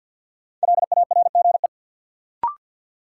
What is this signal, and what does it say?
Morse code, heard as a keyed tone, about 50 wpm, 700 Hz. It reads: LARGE
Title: Sets of 4 Words - New Word Drill - 50wpm